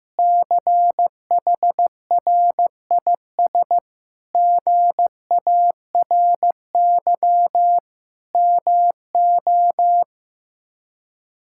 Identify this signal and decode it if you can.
Morse code, heard as a keyed tone, about 15 wpm, 700 Hz. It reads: CHRIS GARY MO
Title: K1USN SST QSO Practice - 15wpm